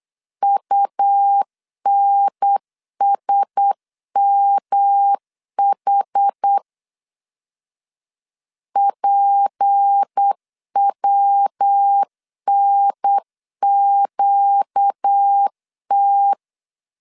Morse code, heard as a keyed tone, about 8 wpm, 800 Hz. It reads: UNSMH PWNQT
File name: LEZIONI / 6 PKX0 / 6 P 8wpm 3025